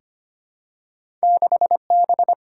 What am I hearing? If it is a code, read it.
Morse code, heard as a keyed tone, about 25 wpm, 700 Hz. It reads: 6B
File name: Two Letter-Number Combinations - 25wpm